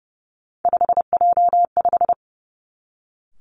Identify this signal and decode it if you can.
Morse code, heard as a keyed tone, about 30 wpm, 700 Hz. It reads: 5J5
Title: Three Challenging Letter-Number Combinations 30wpm